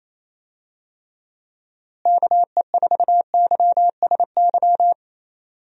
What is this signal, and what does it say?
Morse code, heard as a keyed tone, about 27 wpm, 700 Hz. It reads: KE4YSY
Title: USA Call Signs - 6 Characters - Repeated 2x - 28wpm